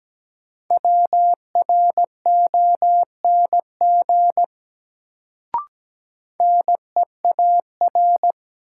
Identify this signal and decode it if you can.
Morse code, heard as a keyed tone, about 17 wpm, 700 Hz. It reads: WRONG NEAR
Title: Sets of 5 Words Encore - New Word Drill - 17wpm